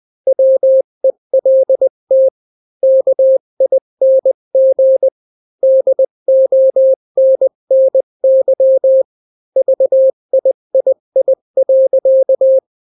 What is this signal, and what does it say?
Morse code, heard as a keyed tone, about 20 wpm, 530 Hz. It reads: WELT KING DONNY VIII.
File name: daily / 18wpm